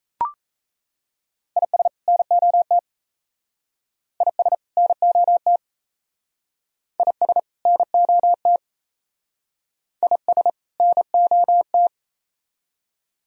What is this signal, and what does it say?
Morse code, heard as a keyed tone, about 35 wpm, 700 Hz. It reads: IS NOT IS NOT IS NOT IS NOT